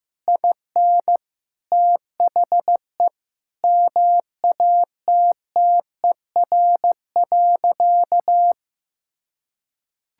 Morse code, heard as a keyed tone, about 15 wpm, 700 Hz. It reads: IN THE MATTER.
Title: Sentences from Top 1000 Words - Review Part 5 - 15wpm